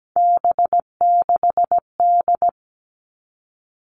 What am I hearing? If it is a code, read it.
Morse code, heard as a keyed tone, about 17 wpm, 700 Hz. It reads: B6D
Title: All Cut Numbers - 3-Digit 17wpm